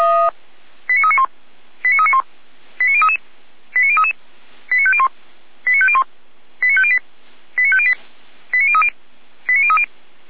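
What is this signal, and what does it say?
Five-tone selective call (ZVEI), heard as quick runs of five tones, tones of about 80 ms. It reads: 89391 89391 89033 89033 89501 89501 89508 89508 89030 89030
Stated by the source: ZVEI five-tone selective call received on SDR.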